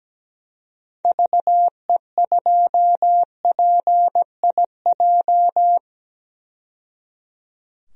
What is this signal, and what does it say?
Morse code, heard as a keyed tone, about 17 wpm, 700 Hz. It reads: VE2PIJ
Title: DXCC-100 Call Signs - 6 Characters - 17wpm